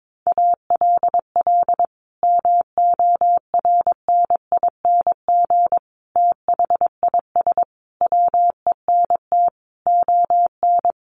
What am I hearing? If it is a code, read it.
Morse code, heard as a keyed tone, about 22 wpm, 700 Hz. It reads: ALL MORNING THIS WENT ON
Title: Sentences from Top 1500 Words 22wpm